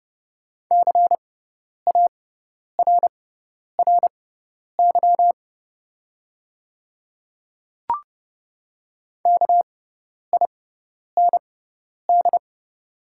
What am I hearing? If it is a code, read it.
Morse code, heard as a keyed tone, about 32 wpm, spaced as for 10 wpm, 700 Hz. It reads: CARRY KIND
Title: 6x Character Spacing - Top 500 Words - 30wpm